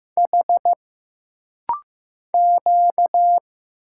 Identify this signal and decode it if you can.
Morse code, heard as a keyed tone, about 15 wpm, 700 Hz. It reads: H Q
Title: Single Letters - Rapid-Fire 15wpm